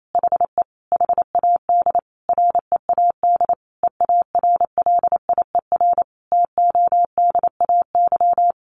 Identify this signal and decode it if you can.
Morse code, heard as a keyed tone, about 28 wpm, 700 Hz. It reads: HE HAD READ EARLIER TODAY
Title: A Sound Of Thunder 28wpm